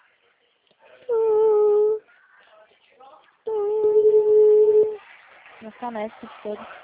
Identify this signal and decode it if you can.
Ringback tone (Europe, 425 Hz), with speech.
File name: out- -2031- - - .48754